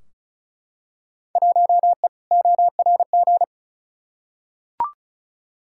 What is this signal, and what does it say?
Morse code, heard as a keyed tone, about 34 wpm, 700 Hz. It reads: 1E ORG
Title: Field Day Exchanges - 35wpm